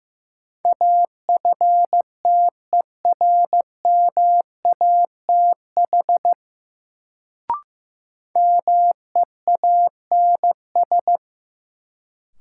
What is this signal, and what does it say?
Morse code, heard as a keyed tone, about 15 wpm, 700 Hz. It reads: AFTERMATH MEANS
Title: Sets of 5 Words - New Word Drill - 15wpm